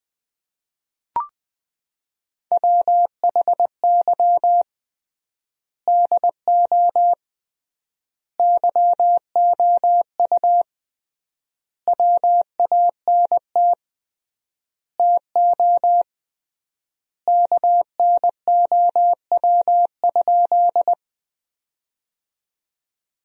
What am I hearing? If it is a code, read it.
Morse code, heard as a keyed tone, about 20 wpm, 700 Hz. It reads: WHY DO YOU WANT TO KNOW?
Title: 3x Word-Spacing - Sentences from Top 200 Words - 20wpm